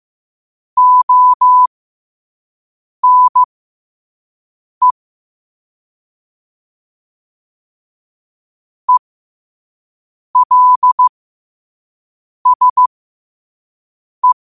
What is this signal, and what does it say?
Morse code, heard as a keyed tone, about 15 wpm, spaced as for 5 wpm, 1000 Hz. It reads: ONE ELSE